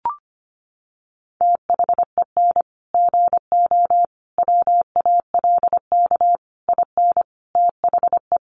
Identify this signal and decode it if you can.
Morse code, heard as a keyed tone, about 25 wpm, 700 Hz. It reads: THEN GO WALK IN THE
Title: Sentences from Top 1300 Words 25wpm